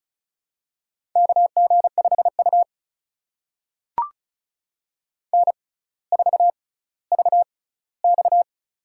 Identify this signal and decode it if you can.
Morse code, heard as a keyed tone, about 36 wpm, spaced as for 18 wpm, 700 Hz. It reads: KG5U N4VX
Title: USA Call Signs - 4 Characters - 6x Character Spacing - 35wpm